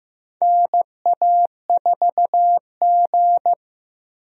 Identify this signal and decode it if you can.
Morse code, heard as a keyed tone, about 15 wpm, 700 Hz. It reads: NA4G
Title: USA Call Signs - 4 Characters - Repeated 3x - 15wpm